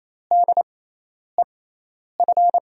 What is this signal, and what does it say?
Morse code, heard as a keyed tone, about 28 wpm, spaced as for 10 wpm, 700 Hz. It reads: DEF
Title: USA Call Signs - 6 Characters - 6x Character Spacing - 28wpm